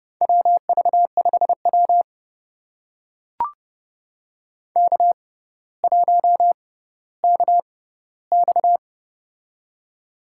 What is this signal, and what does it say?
Morse code, heard as a keyed tone, about 29 wpm, spaced as for 15 wpm, 700 Hz. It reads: WV5W K1KX
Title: USA Call Signs - 4 Characters - 6x Character Spacing - 30wpm